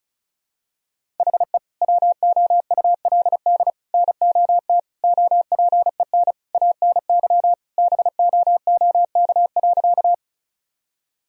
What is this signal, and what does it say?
Morse code, heard as a keyed tone, about 34 wpm, 700 Hz. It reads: HE WOULD NOT OPEN ANY BOOK.
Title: Sentences from Top 1000 Words - Review Part 1 - 35wpm